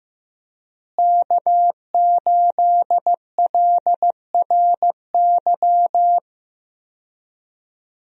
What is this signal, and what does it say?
Morse code, heard as a keyed tone, about 15 wpm, 700 Hz. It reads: K8LRY
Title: USA Call Signs - 5 Characters - Repeated 3x - 15wpm